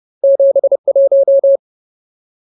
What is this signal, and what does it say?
Morse code, heard as a keyed tone, about 30 wpm, 550 Hz. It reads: 71
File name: Ultimate Word List / 30WPM Ultimate